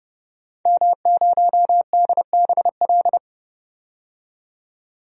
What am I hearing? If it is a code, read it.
Morse code, heard as a keyed tone, about 32 wpm, 700 Hz. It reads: M0DBL
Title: DXCC-100 Call Signs - 5 Characters - 30wpm